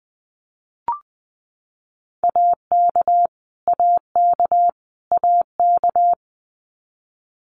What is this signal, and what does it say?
Morse code, heard as a keyed tone, about 20 wpm, 700 Hz. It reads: AK AK AK
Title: US States Repeated 3x 20wpm